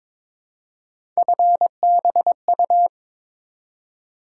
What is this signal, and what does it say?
Morse code, heard as a keyed tone, about 22 wpm, 700 Hz. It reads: FBU
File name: practice / 22wpm / Three Letter-Number Combinations - 22wpm